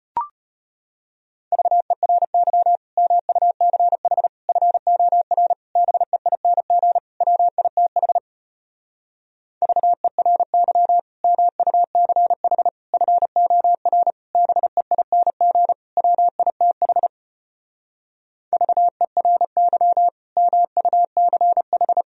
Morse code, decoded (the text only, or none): VERY MUCH FOR BEING WITH VERY MUCH FOR BEING WITH VERY MUCH